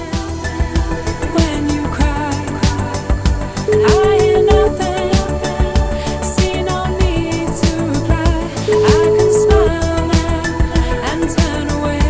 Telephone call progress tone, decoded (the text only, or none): ringback tone